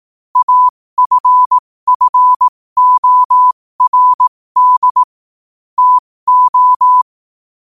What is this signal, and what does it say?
Morse code, heard as a keyed tone, about 18 wpm, spaced as for 15 wpm, 1000 Hz. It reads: AFFORD TO
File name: qotdfiles / 15wpm